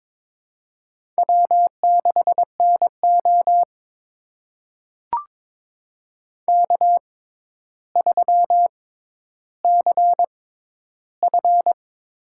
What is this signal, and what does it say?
Morse code, heard as a keyed tone, about 21 wpm, spaced as for 11 wpm, 700 Hz. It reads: W6NO K3CF
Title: USA Call Signs - 6x Character Spacing - 22wpm